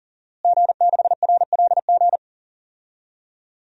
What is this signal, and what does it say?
Morse code, heard as a keyed tone, about 40 wpm, 700 Hz. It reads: G6RLG
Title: DXCC-100 Call Signs - 5 Characters - Repeated 3x - 40wpm